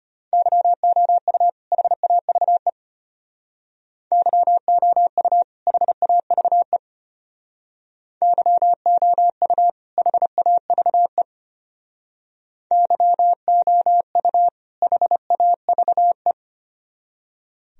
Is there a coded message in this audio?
Yes: Morse code from 0.3 to 16.3 s, heard as a keyed tone, about 32 wpm, 700 Hz. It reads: YOU HAVE YOU HAVE YOU HAVE YOU HAVE